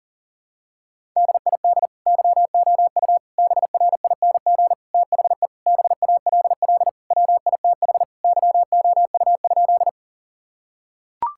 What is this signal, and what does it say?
Morse code, heard as a keyed tone, about 37 wpm, 700 Hz. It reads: DID YOU BRING THE BALL WITH YOU?
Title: Sentences from Top 1000 Words - Review Part 6 - 40wpm